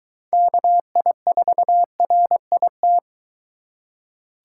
Morse code, heard as a keyed tone, about 22 wpm, 700 Hz. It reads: KI4RIT